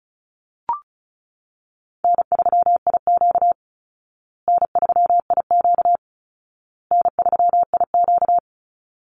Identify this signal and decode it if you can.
Morse code, heard as a keyed tone, about 34 wpm, 700 Hz. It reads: N3IQ N3IQ N3IQ
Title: Call Signs Repeated 2x - 35wpm